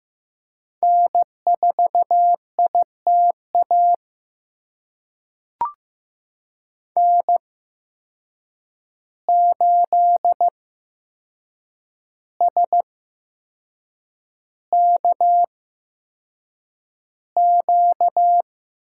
Morse code, heard as a keyed tone, about 15 wpm, spaced as for 6 wpm, 700 Hz. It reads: N4ITA N8SKQ
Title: USA Call Signs - 5 Characters - 8x Character Spacing - 15wpm